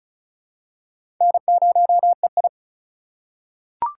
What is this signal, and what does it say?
Morse code, heard as a keyed tone, about 36 wpm, 700 Hz. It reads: N0EI
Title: USA Call Signs - 4 Characters - 3x Character Spacing - 35wpm